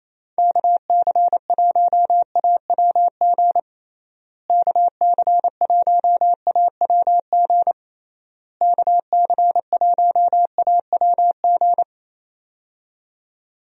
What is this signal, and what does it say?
Morse code, heard as a keyed tone, about 27 wpm, 700 Hz. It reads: KC1AWG KC1AWG KC1AWG
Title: USA Call Signs - 6 Characters - Repeated 3x - 28wpm